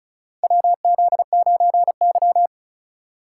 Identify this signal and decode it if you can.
Morse code, heard as a keyed tone, about 36 wpm, 700 Hz. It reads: WZ9Y
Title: USA Call Signs - 4 Characters - 3x Character Spacing - 35wpm